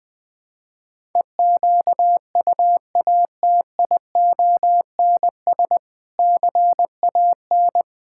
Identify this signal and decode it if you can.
Morse code, heard as a keyed tone, about 20 wpm, 700 Hz. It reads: EQUATIONS CAN
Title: Build A Phrase - 5-Word Phrases - 20wpm